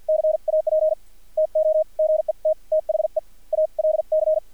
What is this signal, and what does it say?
Morse code, heard as a keyed tone, about 50 wpm, spaced as for 43 wpm, 630 Hz. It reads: CAP TO GET THE ALC
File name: archived / CRSnet